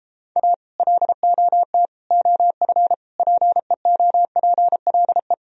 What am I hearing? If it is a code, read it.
Morse code, heard as a keyed tone, about 33 wpm, 700 Hz. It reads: A LOT OF PEOPLE